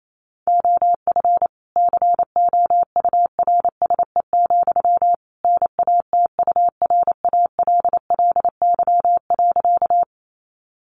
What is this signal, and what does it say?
Morse code, heard as a keyed tone, about 28 wpm, 700 Hz. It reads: OF COURSE, NATURALLY.
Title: Sentences from Top 1300 Words 28wpm